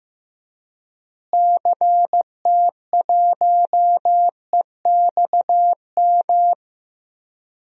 Morse code, heard as a keyed tone, about 15 wpm, 700 Hz. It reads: CT1EXM
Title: DXCC-100 Call Signs - 6 Characters - Repeated 2x - 15wpm